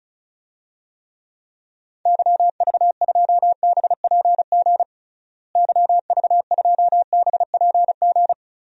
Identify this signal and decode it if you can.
Morse code, heard as a keyed tone, about 34 wpm, 700 Hz. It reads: YV2BPG YV2BPG
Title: DXCC-100 Call Signs - 6 Characters - Repeated 2x - 35wpm